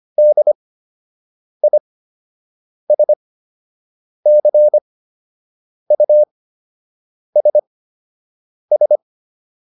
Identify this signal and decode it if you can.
Morse code, heard as a keyed tone, about 25 wpm, spaced as for 7 wpm, 600 Hz. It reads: DISCUSS